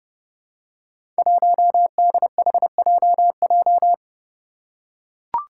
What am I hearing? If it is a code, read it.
Morse code, heard as a keyed tone, about 30 wpm, 700 Hz. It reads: 1DHJJ
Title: Five Challenging Letter-Number Combinations 30wpm